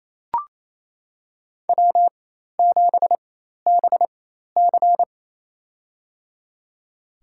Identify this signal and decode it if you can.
Morse code, heard as a keyed tone, about 27 wpm, spaced as for 13 wpm, 700 Hz. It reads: W7BC
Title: USA Call Signs - 4x Character Spacing - 28wpm